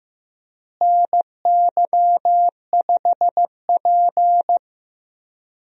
Morse code, heard as a keyed tone, about 15 wpm, 700 Hz. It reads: NY5P
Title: USA Call Signs - 4 Characters - Repeated 2x - 15wpm